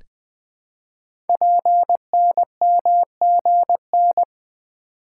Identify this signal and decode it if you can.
Morse code, heard as a keyed tone, about 20 wpm, 700 Hz. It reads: PNMGN
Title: Code Groups - 5 Letters - 20wpm